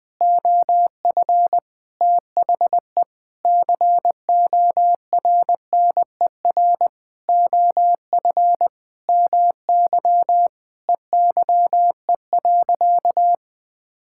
Morse code, heard as a keyed tone, about 20 wpm, 700 Hz. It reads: OF THE CORNER OF MY EYE.
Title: Sentences with 7-Word Phrases - 20wpm